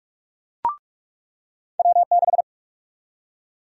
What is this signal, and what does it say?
Morse code, heard as a keyed tone, about 46 wpm, 700 Hz. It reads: W6
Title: USA Call Sign Prefixes - 45wpm